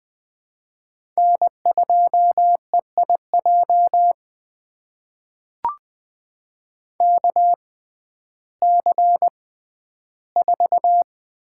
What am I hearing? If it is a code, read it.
Morse code, heard as a keyed tone, about 20 wpm, spaced as for 12 wpm, 700 Hz. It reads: N2EIJ KC4
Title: USA Call Signs - 5 Characters - 6x Character Spacing - 20wpm